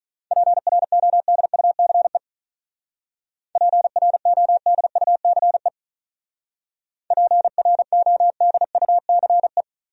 Morse code, heard as a keyed tone, about 42 wpm, 700 Hz. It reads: PRODUCE PRODUCE PRODUCE
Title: Speed-Racing Top 500 Words 35wpm